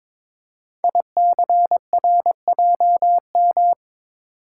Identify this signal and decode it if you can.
Morse code, heard as a keyed tone, about 22 wpm, 700 Hz. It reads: ICRJM
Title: Code Groups - 5 Letters - 22wpm